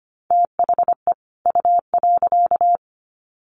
Morse code, heard as a keyed tone, about 25 wpm, 700 Hz. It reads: THE U.
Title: Sets of 2 Words 25wpm